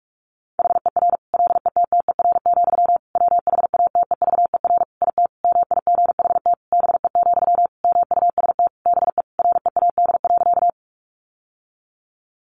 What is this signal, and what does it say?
Morse code, heard as a keyed tone, about 44 wpm, 700 Hz. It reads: HER LETTER, WHATEVER IT MIGHT BE, MUST BE READ.
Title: Sentences from Top 1100 Words 45wpm